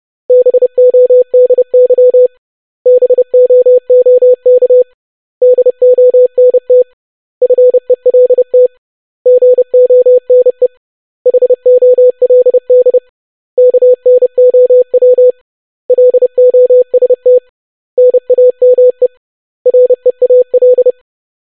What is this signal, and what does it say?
Morse code, heard as a keyed tone, about 30 wpm, 500 Hz. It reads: BODY BOOK DONT FELT GONE HOLD KNOW LOST NAME REAL
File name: basic500-25wpm